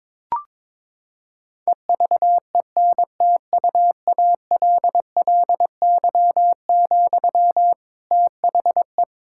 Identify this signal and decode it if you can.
Morse code, heard as a keyed tone, about 22 wpm, 700 Hz. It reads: EVENTUALLY, THE
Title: Sentences from Top 1000 Words - Review Part 6 - 22wpm